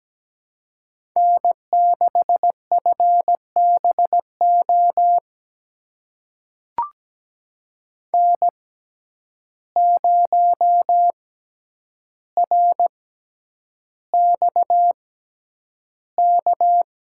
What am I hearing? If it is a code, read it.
Morse code, heard as a keyed tone, about 17 wpm, spaced as for 9 wpm, 700 Hz. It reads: N6FBO N0RXK
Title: USA Call Signs - 5 Characters - 6x Character Spacing - 17wpm